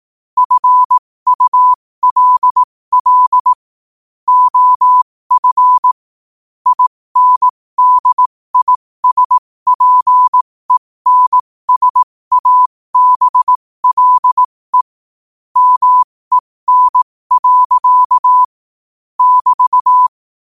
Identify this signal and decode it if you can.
Morse code, heard as a keyed tone, about 18 wpm, spaced as for 15 wpm, 1000 Hz. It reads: FULL OF INDISPENSABLE MEN. <BT>
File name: 15wpm